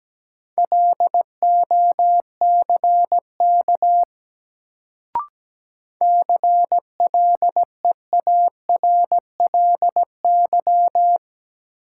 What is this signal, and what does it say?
Morse code, heard as a keyed tone, about 17 wpm, 700 Hz. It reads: LOCK CLEARLY
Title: Sentences from Top 1300 Words - New Word Drill - 17wpm